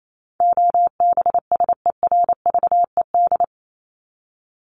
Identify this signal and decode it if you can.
Morse code, heard as a keyed tone, about 28 wpm, 700 Hz. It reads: OBSERVED
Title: ED Suffix - Top 100 Words with Base Words 28wpm